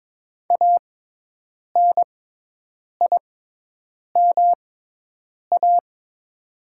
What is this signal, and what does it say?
Morse code, heard as a keyed tone, about 21 wpm, spaced as for 8 wpm, 700 Hz. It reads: ANIMA
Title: 6x Character Spacing - Top 300 Words - 22wpm